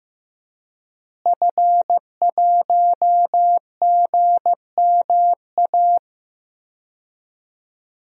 Morse code, heard as a keyed tone, about 15 wpm, 700 Hz. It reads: F1GMA
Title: DXCC-100 Call Signs - 5 Characters - Repeated 3x - 15wpm